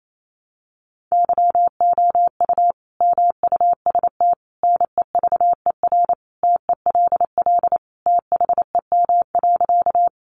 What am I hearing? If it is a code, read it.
Morse code, heard as a keyed tone, about 28 wpm, 700 Hz. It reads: YOU MUST NEVER TELL THEM.
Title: Sentences from Top 900 Words 28wpm